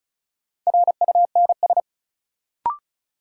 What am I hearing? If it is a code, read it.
Morse code, heard as a keyed tone, about 36 wpm, 700 Hz. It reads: RUNS
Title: Sentences from Top 600 Words - New Word Drill - 35wpm